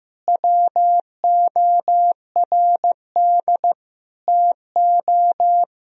Morse code, heard as a keyed tone, about 15 wpm, 700 Hz. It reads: WORD TO
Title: Sentences from Top 1000 Words - Review Part 1 - 15wpm